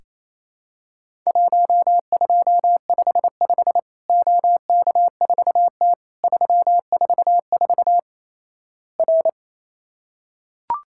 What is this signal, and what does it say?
Morse code, heard as a keyed tone, about 27 wpm, 700 Hz. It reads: 1255 OK4T 344 R
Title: Worked All Europe DX Contest - QTC Exchange - 28wpm